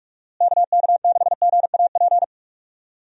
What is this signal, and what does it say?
Morse code, heard as a keyed tone, about 44 wpm, 700 Hz. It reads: KK6GAP
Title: USA Call Signs - 6 Characters - Repeated 3x - 45wpm